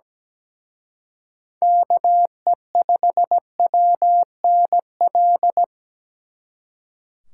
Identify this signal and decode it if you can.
Morse code, heard as a keyed tone, about 20 wpm, 700 Hz. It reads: KE5WNL
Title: Speed Racing - USA Call Signs 17wpm